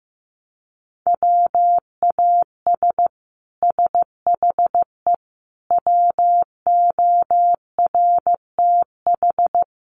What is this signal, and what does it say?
Morse code, heard as a keyed tone, about 15 wpm, 700 Hz. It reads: WAS SHE WORTH
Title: Sentences from Top 1200 Words 15wpm